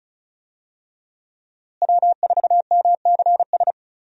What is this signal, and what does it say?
Morse code, heard as a keyed tone, about 34 wpm, 700 Hz. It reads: W4MCS
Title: USA Call Signs - 5 Characters - Repeated 3x - 35wpm